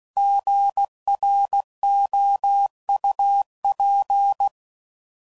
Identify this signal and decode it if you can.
Morse code, heard as a keyed tone, about 16 wpm, 800 Hz. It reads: GROUP
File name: Prefixes / By First Letter / Gs